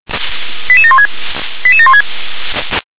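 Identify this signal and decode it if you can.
Five-tone selective call (ZVEI), heard as quick runs of five tones, tones of about 70 ms. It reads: 88716 88716
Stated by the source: ZVEI five-tone selective call received on an SDR.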